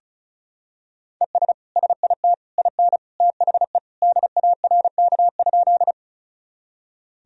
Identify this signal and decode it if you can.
Morse code, heard as a keyed tone, about 36 wpm, 700 Hz. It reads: ES SIT IN THE DARK?
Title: Sentences from Top 500 Words and Abbreviations 35wpm